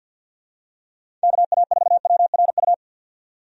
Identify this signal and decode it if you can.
Morse code, heard as a keyed tone, about 48 wpm, 700 Hz. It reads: KA4WRU
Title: USA Call Signs - 6 Characters - 50wpm